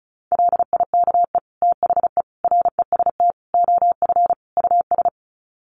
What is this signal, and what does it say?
Morse code, heard as a keyed tone, about 34 wpm, 700 Hz. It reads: LIKE THE REST OF US
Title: Sets of 5 Words Encore2 35wpm